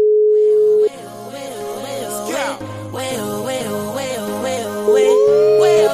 Ringback tone (Europe, 425 Hz), with noise.